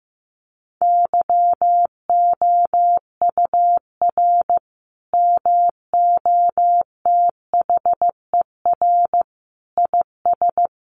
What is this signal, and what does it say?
Morse code, heard as a keyed tone, about 15 wpm, 700 Hz. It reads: YOUR MOTHER IS
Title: Sentences from Top 1400 Words 15wpm